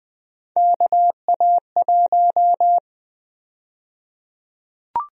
Morse code, heard as a keyed tone, about 20 wpm, 700 Hz. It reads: KA1
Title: USA Call Sign Prefixes - 20wpm